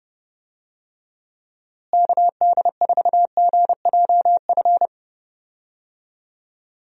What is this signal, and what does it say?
Morse code, heard as a keyed tone, about 30 wpm, 700 Hz. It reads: KD4GJF
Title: USA Call Signs - 6 Characters - Repeated 3x - 30wpm